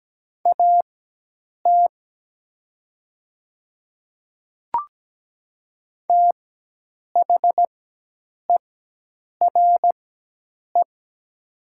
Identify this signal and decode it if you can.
Morse code, heard as a keyed tone, about 17 wpm, spaced as for 8 wpm, 700 Hz. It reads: AT THERE
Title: 4x Character Spacing - Top 100 Words - 17wpm